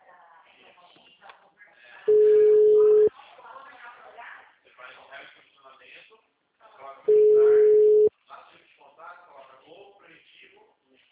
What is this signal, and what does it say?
Ringback tone (Europe, 425 Hz), with speech.